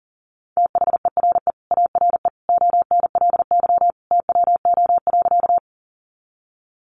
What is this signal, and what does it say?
Morse code, heard as a keyed tone, about 40 wpm, 700 Hz. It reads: THERE ARE ONLY TWO.
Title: Sentences from Top 200 Words 40wpm